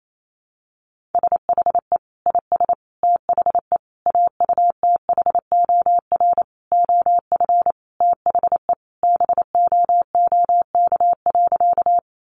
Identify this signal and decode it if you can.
Morse code, heard as a keyed tone, about 28 wpm, 700 Hz. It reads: SHE IS THE AUTHOR OF THE BOOK.
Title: Sentences from Top 1200 Words 28wpm